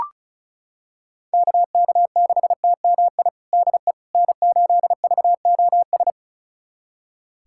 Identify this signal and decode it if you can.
Morse code, heard as a keyed tone, about 36 wpm, 700 Hz. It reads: KK6TMI DE N8VOS
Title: USA Callsign DE Callsign 35wpm